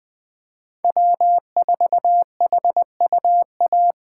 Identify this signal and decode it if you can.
Morse code, heard as a keyed tone, about 20 wpm, 700 Hz. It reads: W4HUA
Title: USA Call Signs - 5 Characters - 4x Character Spacing - 20wpm